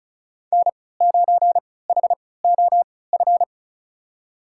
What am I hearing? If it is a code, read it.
Morse code, heard as a keyed tone, about 36 wpm, spaced as for 20 wpm, 700 Hz. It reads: N9HOF
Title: USA Call Signs - 5 Characters - 3x Character Spacing - 35wpm